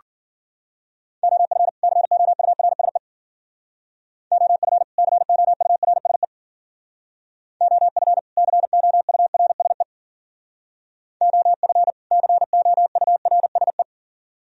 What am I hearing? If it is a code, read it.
Morse code, heard as a keyed tone, about 48 wpm, 700 Hz. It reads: OF COURSE OF COURSE OF COURSE OF COURSE
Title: Speed-Racing - Sets of 2 Words 40wpm